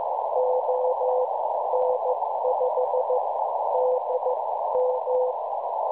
Morse code, heard as a keyed tone, about 14 wpm, spaced as for 11 wpm, 510 Hz, with noise. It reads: ON5DM
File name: cwronde / websdr recording .6kHz